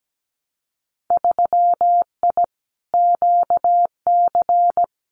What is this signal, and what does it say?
Morse code, heard as a keyed tone, about 17 wpm, 700 Hz. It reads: 3I QC
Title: Winter Field Day Exchanges Encore 17wpm